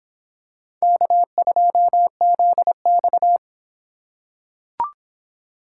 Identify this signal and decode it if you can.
Morse code, heard as a keyed tone, about 25 wpm, 700 Hz. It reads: K2ZX